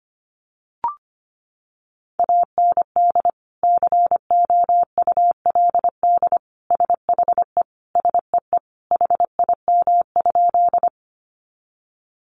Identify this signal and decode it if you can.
Morse code, heard as a keyed tone, about 25 wpm, 700 Hz. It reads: AND COULD SHE SEE HIM?
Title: Sentences from Top 100 Words 25wpm